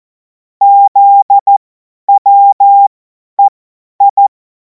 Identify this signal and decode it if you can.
Morse code, heard as a keyed tone, about 14 wpm, spaced as for 10 wpm, 800 Hz. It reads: ZWEI
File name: Morsekurs / Witze / Witz 23. Juni 2026 50 F70